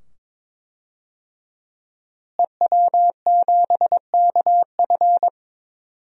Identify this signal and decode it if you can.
Morse code, heard as a keyed tone, about 22 wpm, 700 Hz. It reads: EW7KF
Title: DXCC-100 Call Signs - 5 Characters - Repeated 2x - 22wpm